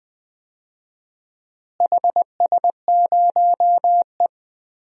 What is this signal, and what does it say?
Morse code, heard as a keyed tone, about 19 wpm, 690 Hz. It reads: HS0E
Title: DXCC-100 Call Signs - 4 Characters - Repeated 3x - 20wpm